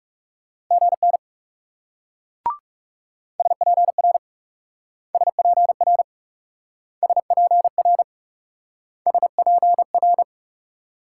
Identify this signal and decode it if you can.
Morse code, heard as a keyed tone, about 38 wpm, 700 Hz. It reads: GN SPR SPR SPR SPR